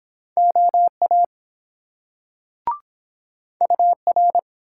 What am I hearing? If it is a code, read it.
Morse code, heard as a keyed tone, about 23 wpm, 700 Hz. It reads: OA UR